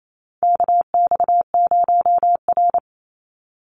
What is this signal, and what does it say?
Morse code, heard as a keyed tone, about 28 wpm, 700 Hz. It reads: KX0R
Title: NA SOTA Activator Callsigns 28wpm